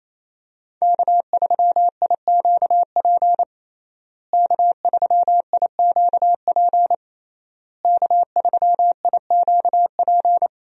Morse code, heard as a keyed tone, about 27 wpm, 700 Hz. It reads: K3IQP K3IQP K3IQP
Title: USA Call Signs - 5 Characters - Repeated 3x - 28wpm